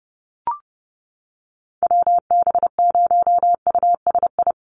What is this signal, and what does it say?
Morse code, heard as a keyed tone, about 30 wpm, 700 Hz. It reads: WB0USI
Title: NA SOTA Activator Callsigns 30wpm